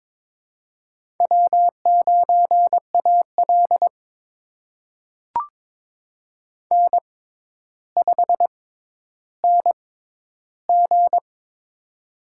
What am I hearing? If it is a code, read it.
Morse code, heard as a keyed tone, about 22 wpm, spaced as for 11 wpm, 700 Hz. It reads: W9AL N5NG
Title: USA Call Signs - 4 Characters - 6x Character Spacing - 22wpm